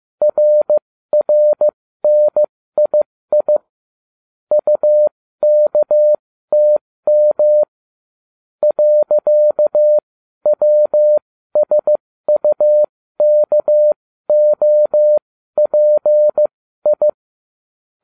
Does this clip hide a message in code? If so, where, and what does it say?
Yes: Morse code from 0.2 to 17.1 s, heard as a keyed tone, about 15 wpm, spaced as for 13 wpm, 600 Hz. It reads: RRNII UKTM .WSUKOPI